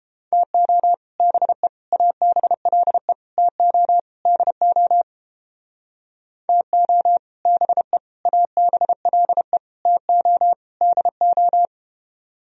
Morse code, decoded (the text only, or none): TO BE ABLE TO DO TO BE ABLE TO DO